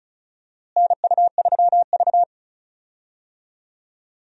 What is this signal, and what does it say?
Morse code, heard as a keyed tone, about 36 wpm, 700 Hz. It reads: NU3V
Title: USA Call Signs - 4 Characters - Repeated 3x - 35wpm